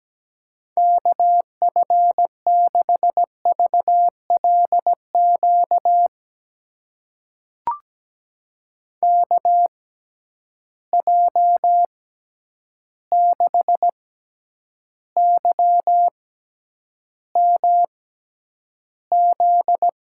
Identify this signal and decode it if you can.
Morse code, heard as a keyed tone, about 17 wpm, spaced as for 9 wpm, 700 Hz. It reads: KF6VLQ KJ6YMZ
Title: USA Call Signs - 6 Characters - 6x Character Spacing - 17wpm